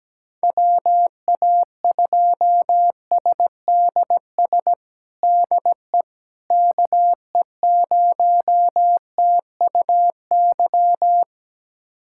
Morse code, heard as a keyed tone, about 17 wpm, 700 Hz. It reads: WA2SDS DE KE0TUY
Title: USA Callsign DE Callsign - 6 Characters - 17wpm